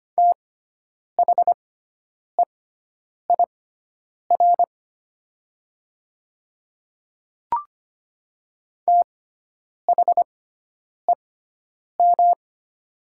Morse code, heard as a keyed tone, about 25 wpm, spaced as for 9 wpm, 700 Hz. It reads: THEIR THEM
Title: 6x Character Spacing - Top 100 Words - 25wpm